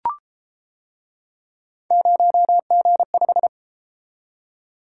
Morse code, heard as a keyed tone, about 31 wpm, 700 Hz. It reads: 0G5